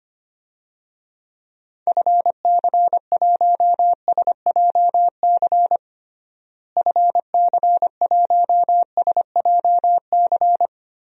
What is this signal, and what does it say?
Morse code, heard as a keyed tone, about 25 wpm, 700 Hz. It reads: FC1SJC FC1SJC
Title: DXCC-100 Call Signs - 6 Characters - Repeated 3x - 25wpm